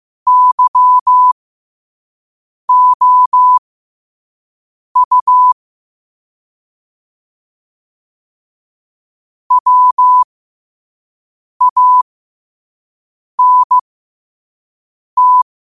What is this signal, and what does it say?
Morse code, heard as a keyed tone, about 15 wpm, spaced as for 5 wpm, 1000 Hz. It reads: YOU WANT
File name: qotdfiles / 5wpm